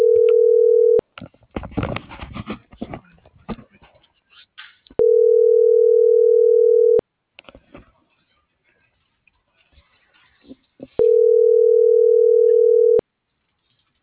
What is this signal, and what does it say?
Ringback tone (North America), 440 + 480 Hz.